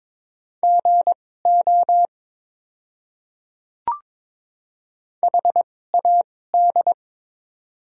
Morse code, heard as a keyed tone, about 21 wpm, spaced as for 16 wpm, 700 Hz. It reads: GO HAD
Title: 2x Character Spacing - Top 100 Words - 22wpm